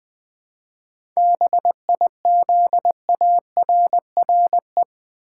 Morse code, heard as a keyed tone, about 20 wpm, 700 Hz. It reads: BIZARRE
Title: 7 Letter Words - First 3 Letters Given - 20wpm